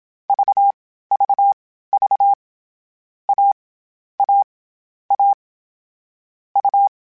Morse code, heard as a keyed tone, about 27 wpm, spaced as for 15 wpm, 800 Hz. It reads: VVV A A A U